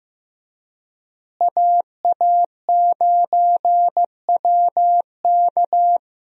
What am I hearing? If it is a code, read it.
Morse code, heard as a keyed tone, about 16 wpm, 700 Hz. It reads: AA9WK
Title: USA Call Signs - 5 Characters - Repeated 3x - 15wpm